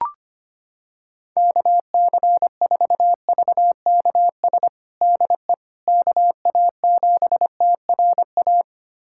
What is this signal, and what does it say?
Morse code, heard as a keyed tone, about 25 wpm, 700 Hz. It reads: KC4VKS DE KA7TRA
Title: USA Callsign DE Callsign 25wpm